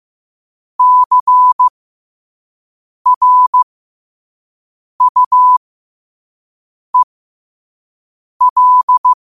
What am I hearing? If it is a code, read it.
Morse code, heard as a keyed tone, about 15 wpm, spaced as for 5 wpm, 1000 Hz. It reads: CRUEL
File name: qotdfiles / 5wpm